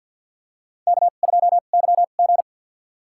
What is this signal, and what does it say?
Morse code, heard as a keyed tone, about 50 wpm, spaced as for 36 wpm, 700 Hz. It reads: K2YG
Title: USA Call Signs - 2x Character Spacing - 50wpm